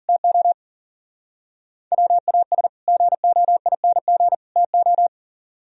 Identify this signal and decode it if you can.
Morse code, heard as a keyed tone, about 38 wpm, 700 Hz. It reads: TO WAS GOING TO
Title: Speed-Racing - Sets of 3 Words 40wpm